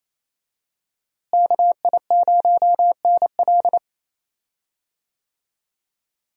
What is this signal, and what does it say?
Morse code, heard as a keyed tone, about 27 wpm, 700 Hz. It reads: KI0NL
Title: USA Call Signs - 5 Characters - Repeated 3x - 28wpm